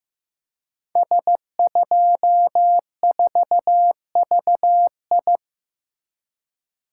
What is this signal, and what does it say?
Morse code, heard as a keyed tone, about 15 wpm, 700 Hz. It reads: S24VI
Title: Code Groups - 5 Letter:Numbers - 15wpm